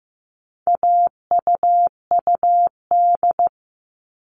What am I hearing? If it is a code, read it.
Morse code, heard as a keyed tone, about 15 wpm, 700 Hz. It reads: AUUD
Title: All Cut Numbers - 4-Digit 15wpm